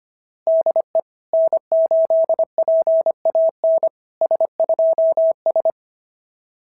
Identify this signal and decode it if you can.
Morse code, heard as a keyed tone, about 24 wpm, 650 Hz. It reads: DE N8PAN S2S
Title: SOTA QSO Practice 25wpm